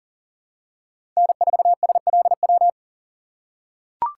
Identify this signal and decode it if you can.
Morse code, heard as a keyed tone, about 38 wpm, 700 Hz. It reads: N4SLW
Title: USA Call Signs - 2x Character Spacing - 40wpm